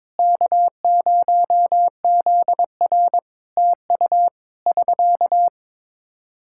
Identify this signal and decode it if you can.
Morse code, heard as a keyed tone, about 23 wpm, 700 Hz. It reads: K0ZR TU <SK>
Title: K1USN SST QSO Practice 22wpm